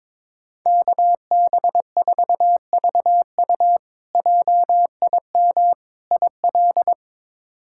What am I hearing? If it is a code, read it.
Morse code, heard as a keyed tone, about 22 wpm, 700 Hz. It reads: KB4VU JIM IL
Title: K1USN SST QSO Practice - 22wpm